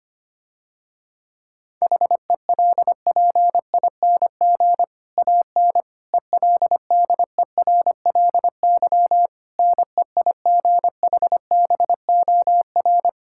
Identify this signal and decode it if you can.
Morse code, heard as a keyed tone, about 25 wpm, 700 Hz. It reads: HELPING AN ELDERLY NEIGHBOR
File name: practice / 25wpm / Difficult Sentences with Hints - 25wpm